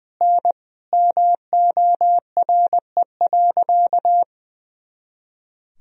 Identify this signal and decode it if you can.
Morse code, heard as a keyed tone, about 20 wpm, 700 Hz. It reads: N MORE.
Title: Sentences from Top 300 Words and Abbreviations 20wpm